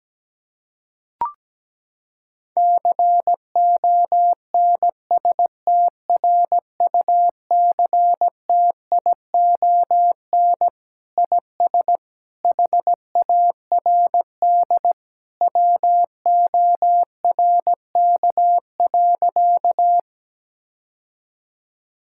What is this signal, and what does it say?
Morse code, heard as a keyed tone, about 17 wpm, 700 Hz. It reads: CONSTRUCTION IS HARD WORK.
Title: Sentences from Top 1600 Words 17wpm